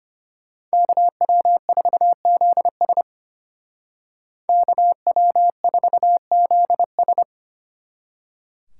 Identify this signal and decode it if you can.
Morse code, heard as a keyed tone, about 31 wpm, 700 Hz. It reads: KW4ZS KW4ZS
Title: Speed Racing - USA Call Signs 25wpm